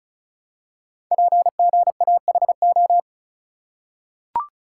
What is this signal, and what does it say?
Morse code, heard as a keyed tone, about 36 wpm, 700 Hz. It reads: PGAHO
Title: Code Groups - 5 Letters - 35wpm